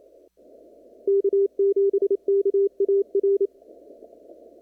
Morse code, heard as a keyed tone, about 28 wpm, 400 Hz. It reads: K7KAR
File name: CW / kt5j wpx cw-op-k5pi